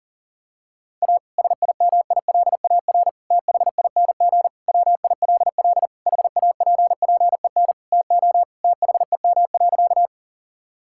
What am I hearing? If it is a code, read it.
Morse code, heard as a keyed tone, about 37 wpm, 700 Hz. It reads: A SIMILAR THING WILL HAPPEN TO THEM.
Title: Sentences from Top 1000 Words - Review Part 5 - 40wpm